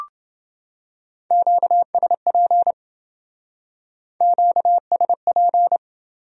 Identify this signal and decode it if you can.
Morse code, heard as a keyed tone, about 27 wpm, 700 Hz. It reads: QSP QSP